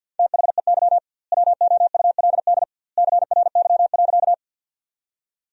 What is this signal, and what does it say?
Morse code, heard as a keyed tone, about 46 wpm, 700 Hz. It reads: THEY WOULD CRY.
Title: 2x Word-Spacing - Sentences from Top 500 Words - 50wpm